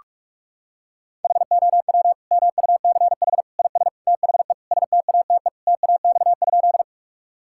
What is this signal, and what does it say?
Morse code, heard as a keyed tone, about 41 wpm, 700 Hz. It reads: HOW MUCH IS THE STATE TAX?
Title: Sentences from Top 1000 Words - Review Part 5 - 45wpm